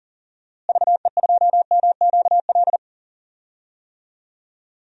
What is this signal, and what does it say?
Morse code, heard as a keyed tone, about 40 wpm, 700 Hz. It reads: VE2MQL
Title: DXCC-100 Call Signs - 6 Characters - Repeated 3x - 40wpm